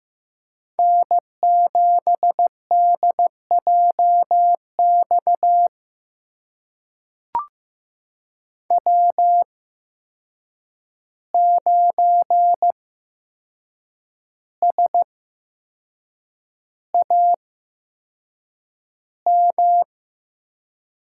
Morse code, heard as a keyed tone, about 15 wpm, spaced as for 6 wpm, 700 Hz. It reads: N7DJX W9SAM
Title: USA Call Signs - 5 Characters - 8x Character Spacing - 15wpm